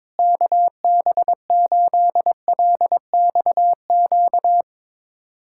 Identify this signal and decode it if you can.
Morse code, heard as a keyed tone, about 22 wpm, 700 Hz. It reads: KB8LXQ
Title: USA Callsign DE Callsign - 6 Characters - 22wpm